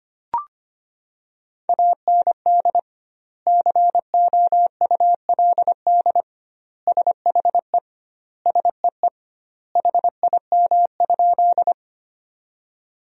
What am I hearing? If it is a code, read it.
Morse code, heard as a keyed tone, about 25 wpm, 700 Hz. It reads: AND COULD SHE SEE HIM?
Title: 2x Word-Spacing - Sentences from Top 100 Words - 25wpm